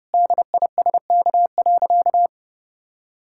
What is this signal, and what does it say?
Morse code, heard as a keyed tone, about 30 wpm, 700 Hz. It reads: DISK.
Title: Sentences from Top 1000 Words - Review Part 6 - 30wpm